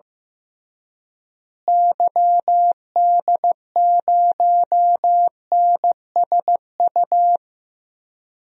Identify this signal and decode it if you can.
Morse code, heard as a keyed tone, about 15 wpm, 700 Hz. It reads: YD0NSU
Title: DXCC-100 Call Signs - 6 Characters - Repeated 3x - 15wpm